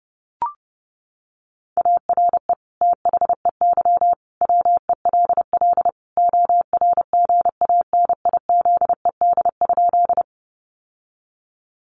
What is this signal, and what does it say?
Morse code, heard as a keyed tone, about 30 wpm, 700 Hz. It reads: ARE THEY WELL ORGANIZED?
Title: Sentences from Top 1300 Words 30wpm